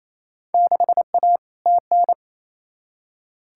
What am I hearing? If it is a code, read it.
Morse code, heard as a keyed tone, about 28 wpm, 700 Hz. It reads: 6A TN
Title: Field Day Exchanges Encore - 28wpm